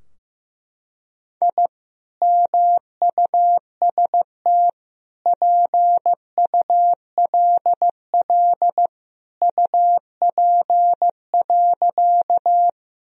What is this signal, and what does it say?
Morse code, heard as a keyed tone, about 16 wpm, 700 Hz. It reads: I MUST PULL UP.
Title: Sentences from Top 1000 Words - Review Part 2 - 15wpm